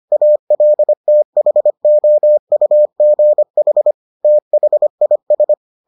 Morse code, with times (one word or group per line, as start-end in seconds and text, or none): 0.1-3.9 ALTHOUGH
4.2-5.5 THIS